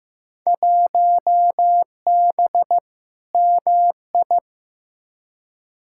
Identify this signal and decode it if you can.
Morse code, heard as a keyed tone, about 15 wpm, 700 Hz. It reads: 1B MI
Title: Field Day Exchanges - 15wpm